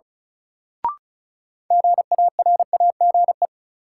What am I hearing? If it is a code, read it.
Morse code, heard as a keyed tone, about 36 wpm, 700 Hz. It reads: GARAGE
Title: Sentences from Top 1600 Words - New Word Drill - 35wpm